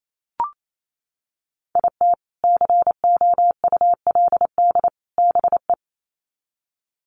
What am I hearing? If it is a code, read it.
Morse code, heard as a keyed tone, about 28 wpm, 700 Hz. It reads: IT COULD BE
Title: Sets of 3 Words 28wpm